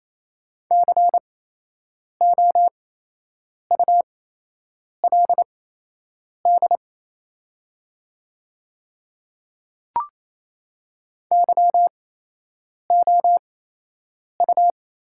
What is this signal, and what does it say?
Morse code, heard as a keyed tone, about 27 wpm, spaced as for 8 wpm, 700 Hz. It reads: COULD YOU
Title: 8x Character Spacing - Top 100 Words - 28wpm